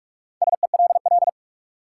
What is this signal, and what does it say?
Morse code, heard as a keyed tone, about 46 wpm, 700 Hz. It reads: SELL
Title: Sentences from Top 1100 Words - New Word Drill Alt - 45wpm